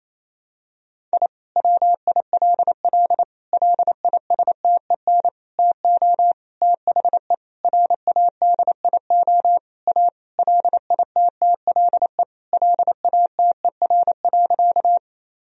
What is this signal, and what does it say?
Morse code, heard as a keyed tone, about 28 wpm, 700 Hz. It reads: I WILL LISTEN TO THE RADIO A LITTLE LATER.
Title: Sentences from Top 1000 Words - Review Part 3 - 28wpm